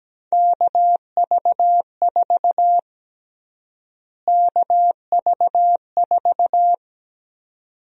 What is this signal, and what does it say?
Morse code, heard as a keyed tone, about 17 wpm, 700 Hz. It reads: KV4 KV4
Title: USA Call Sign Prefixes - Repeated 2x - 17wpm